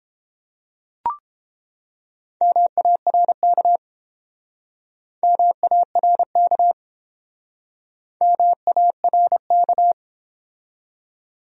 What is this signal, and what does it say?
Morse code, heard as a keyed tone, about 29 wpm, 700 Hz. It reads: MARK MARK MARK